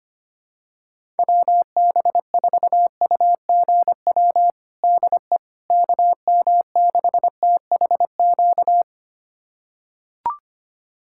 Morse code, heard as a keyed tone, about 25 wpm, 700 Hz. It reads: WB4UGW DE KM6THQ
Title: USA Callsign DE Callsign - 6 Characters - 25wpm